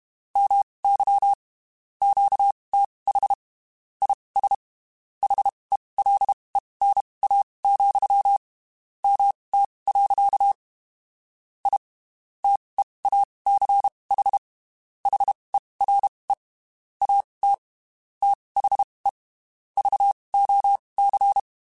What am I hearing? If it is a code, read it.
Morse code, heard as a keyed tone, about 32 wpm, spaced as for 23 wpm, 800 Hz. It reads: MY QTH IS HELENA, MT. I TEACH HERE AT THE VOC
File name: (spaced) / QSOs Spaced-mp3s / WD0BGP 1 @30 sp